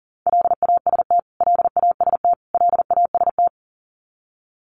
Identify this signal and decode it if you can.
Morse code, heard as a keyed tone, about 40 wpm, 700 Hz. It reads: LAST LAST LAST
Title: Top 300 Words Repeated 40wpm